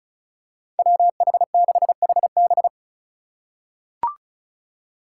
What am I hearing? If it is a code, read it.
Morse code, heard as a keyed tone, about 36 wpm, 700 Hz. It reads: WH6HB
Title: USA Call Signs - 5 Characters - 3x Character Spacing - 35wpm